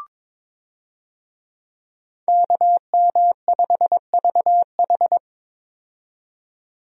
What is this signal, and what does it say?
Morse code, heard as a keyed tone, about 22 wpm, 700 Hz. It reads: KM5VH
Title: USA Call Signs - 5 Characters - 22wpm